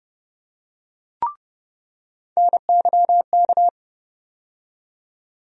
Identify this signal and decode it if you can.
Morse code, heard as a keyed tone, about 31 wpm, 700 Hz. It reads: NYK
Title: USA Call Sign Postfixes - 3 Letters 30wpm